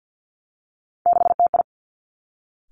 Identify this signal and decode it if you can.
Morse code, heard as a keyed tone, about 50 wpm, 700 Hz. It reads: 6TI
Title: Three Challenging Letter-Number Combinations 50wpm